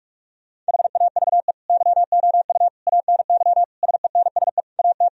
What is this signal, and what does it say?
Morse code, heard as a keyed tone, about 41 wpm, 700 Hz. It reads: HAVE YOU ANY SENSE AT
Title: Sentences from Top 1000 Words - Review Part 4 - 45wpm